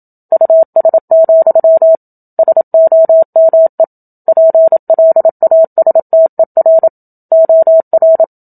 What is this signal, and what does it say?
Morse code, heard as a keyed tone, about 27 wpm, 650 Hz. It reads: US, SOME PLASTER OR
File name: Book Two / 01-Chapter 1 - 27 WPM